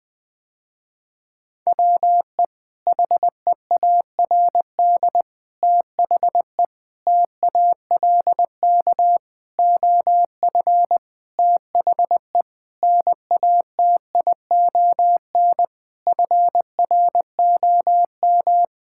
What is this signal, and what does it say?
Morse code, heard as a keyed tone, about 20 wpm, 700 Hz. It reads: WE HEARD THE TALK OF THE NATION FROM
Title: Sentences with 5-Word Phrases - Spoken Beginning - 20wpm